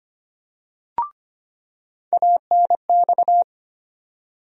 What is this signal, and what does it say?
Morse code, heard as a keyed tone, about 25 wpm, 700 Hz. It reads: ANX
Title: USA Call Sign Postfixes - 3 Letters 25wpm